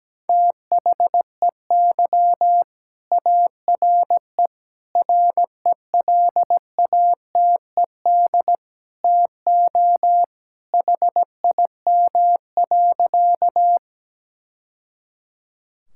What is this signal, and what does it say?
Morse code, heard as a keyed tone, about 17 wpm, 700 Hz. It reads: THEY ARE RELATED TO HIM.
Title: Sentences from Top 1000 Words - Review Part 5 - 17wpm